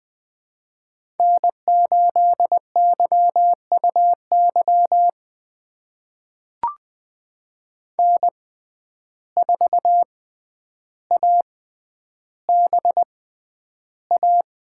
Morse code, heard as a keyed tone, about 20 wpm, spaced as for 10 wpm, 700 Hz. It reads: N8YUY N4ABA
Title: USA Call Signs - 5 Characters - 6x Character Spacing - 20wpm